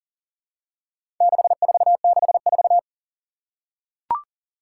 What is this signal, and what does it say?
Morse code, heard as a keyed tone, about 40 wpm, 700 Hz. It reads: 6464
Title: Four Challenging Letter-Number Combinations 40wpm